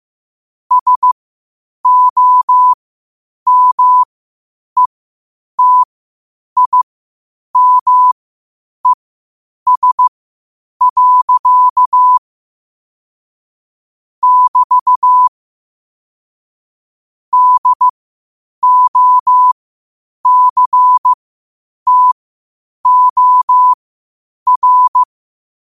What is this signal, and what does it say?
Morse code, heard as a keyed tone, about 15 wpm, spaced as for 8 wpm, 1000 Hz. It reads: SOMETIMES. <BT> DOCTOR
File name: qotdfiles / 8wpm